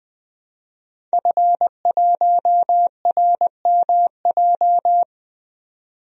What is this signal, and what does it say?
Morse code, heard as a keyed tone, about 20 wpm, 700 Hz. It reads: F1RMJ
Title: DXCC-100 Call Signs - 5 Characters - 20wpm